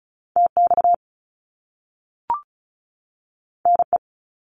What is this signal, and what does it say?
Morse code, heard as a keyed tone, about 36 wpm, 700 Hz. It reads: TX NE
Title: US States 35wpm